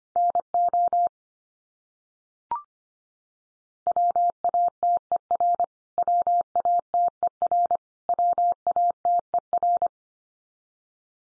Morse code, heard as a keyed tone, about 25 wpm, 700 Hz. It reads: NO WATER WATER WATER
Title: Top 100 Words 25wpm x3